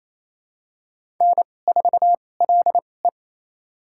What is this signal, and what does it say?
Morse code, heard as a keyed tone, about 28 wpm, spaced as for 20 wpm, 700 Hz. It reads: N4LE
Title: USA Call Signs - 4 Characters - 2x Character Spacing - 28wpm